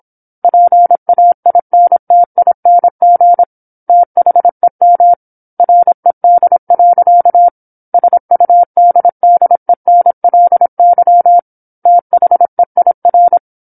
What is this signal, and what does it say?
Morse code, heard as a keyed tone, about 27 wpm, 700 Hz. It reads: PAINTING THEM RED. SUDDENLY THEIR
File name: 08-Chapter-08 26 WPM